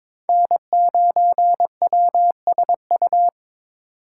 Morse code, heard as a keyed tone, about 22 wpm, 700 Hz. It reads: N9WSU
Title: USA Call Signs - 8x Character Spacing - 22wpm